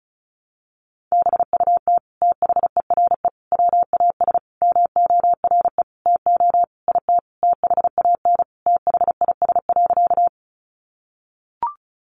Morse code, heard as a keyed tone, about 34 wpm, 700 Hz. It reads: BUT THERE WAS MORE TO IT THAN THIS.
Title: Sentences from Top 100 Words 35wpm